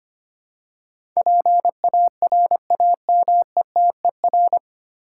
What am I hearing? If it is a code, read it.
Morse code, heard as a keyed tone, about 25 wpm, 700 Hz. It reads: PARAMETER
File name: practice / 25wpm / 9 Letter Words - First 3 Letters Given - 25wpm